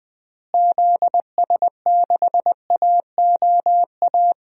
Morse code, heard as a keyed tone, about 20 wpm, 700 Hz. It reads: ZS6AOA
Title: DXCC-100 Call Signs - 6 Characters - Repeated 3x - 20wpm